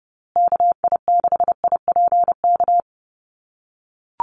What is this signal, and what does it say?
Morse code, heard as a keyed tone, about 30 wpm, 700 Hz. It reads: KI6IPK
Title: USA Call Signs Repeated 2x 30wpm